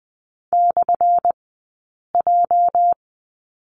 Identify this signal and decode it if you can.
Morse code, heard as a keyed tone, about 20 wpm, 700 Hz. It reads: / J
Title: 204 Lesson 33d - Letter j - Two Characters 20wpm